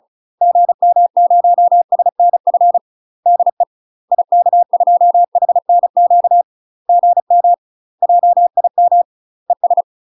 Morse code, heard as a keyed tone, about 35 wpm, 700 Hz. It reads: GM0SNF DE IK2HNQ GM JIM ES
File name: UK MASTER QSOs TRAINING / 35WPM / UK MASTER QSOs 0036